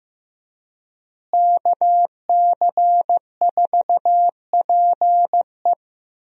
Morse code, heard as a keyed tone, about 15 wpm, 700 Hz. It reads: KC4PE
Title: USA Call Signs - 5 Characters - Repeated 2x - 15wpm